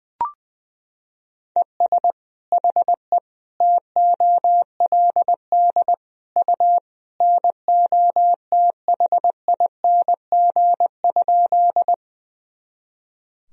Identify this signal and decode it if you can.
Morse code, heard as a keyed tone, about 20 wpm, 700 Hz. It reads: ES HE TOLD U NOTHING?
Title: Sentences from Top 500 Words and Abbreviations 20wpm